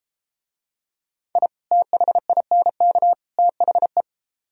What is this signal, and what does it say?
Morse code, heard as a keyed tone, about 33 wpm, 700 Hz. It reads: I THINK THE